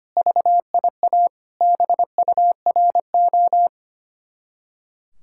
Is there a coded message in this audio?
Yes: Morse code from 0.2 to 3.7 s, heard as a keyed tone, about 32 wpm, 700 Hz. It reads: VIA BURO